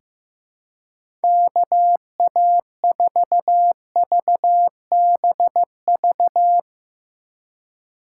Morse code, heard as a keyed tone, about 15 wpm, 700 Hz. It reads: KA4VBV
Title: USA Call Signs - 3x Character Spacing - 15wpm